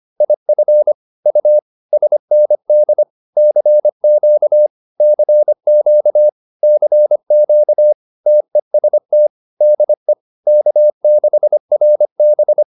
Morse code, heard as a keyed tone, about 25 wpm, 600 Hz. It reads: IF U SND CQ CQ CQ TEST DE K6RB